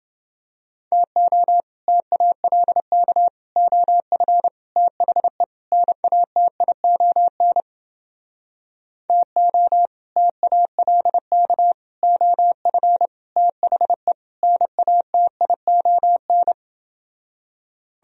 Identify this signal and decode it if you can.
Morse code, heard as a keyed tone, about 29 wpm, 700 Hz. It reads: TO TALK OF THE NATION TO TALK OF THE NATION